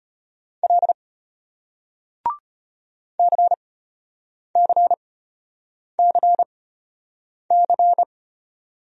Morse code, heard as a keyed tone, about 35 wpm, spaced as for 8 wpm, 700 Hz. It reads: L CCCC